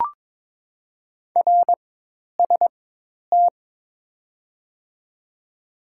Morse code, heard as a keyed tone, about 21 wpm, spaced as for 10 wpm, 700 Hz. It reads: RST
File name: Top 90 QSO Elements - 4x Character Spacing - 22wpm